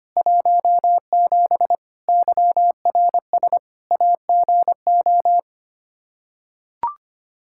Morse code, heard as a keyed tone, about 25 wpm, 700 Hz. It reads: 17 YRS AGO
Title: K1USN SST QSO Practice 25wpm